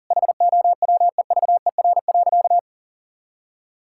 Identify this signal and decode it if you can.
Morse code, heard as a keyed tone, about 40 wpm, 700 Hz. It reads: HOWEVER.
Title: Sentences from Top 1000 Words - Review Part 5 - 40wpm